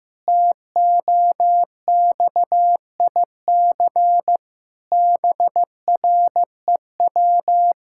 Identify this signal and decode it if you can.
Morse code, heard as a keyed tone, about 16 wpm, 700 Hz. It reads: TOXIC BREW
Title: Pangram Challenge v2 - 15wpm